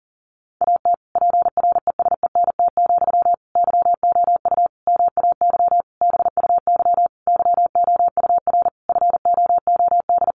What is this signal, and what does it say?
Morse code, heard as a keyed tone, about 40 wpm, 700 Hz. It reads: AT PRESENT, YOU MAY BUY YOUR FOOD
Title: Sentences from Top 800 Words 40wpm